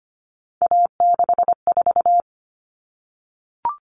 Morse code, heard as a keyed tone, about 25 wpm, 700 Hz. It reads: A64
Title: All Cut Numbers - 3-Digit 25wpm